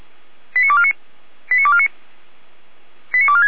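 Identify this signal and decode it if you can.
Five-tone selective call (ZVEI), heard as quick runs of five tones, tones of about 70 ms. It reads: 89249 89249 89246
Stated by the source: ZVEI five-tone call received on SDR by fire brigade Orlamünde (feuerwehr-orlamuende).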